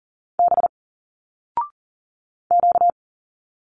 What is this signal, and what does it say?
Morse code, heard as a keyed tone, about 41 wpm, 700 Hz. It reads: B Q
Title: Single Letters 40wpm